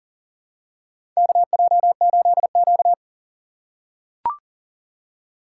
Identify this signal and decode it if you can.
Morse code, heard as a keyed tone, about 40 wpm, 700 Hz. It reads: KJ8Q
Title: USA Call Signs - 3x Character Spacing - 40wpm